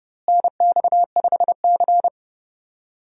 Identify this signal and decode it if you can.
Morse code, heard as a keyed tone, about 32 wpm, 700 Hz. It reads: NX5C
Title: USA Call Signs - 4 Characters - Repeated 2x - 30wpm